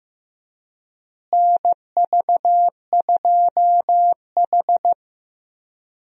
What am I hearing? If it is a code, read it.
Morse code, heard as a keyed tone, about 16 wpm, 700 Hz. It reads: NV2H
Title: USA Call Signs - 4 Characters - Repeated 2x - 15wpm